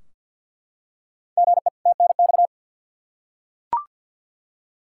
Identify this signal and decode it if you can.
Morse code, heard as a keyed tone, about 52 wpm, 700 Hz. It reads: GE TNX
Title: Practice QSO - Exchange 1 - Beginning - 50wpm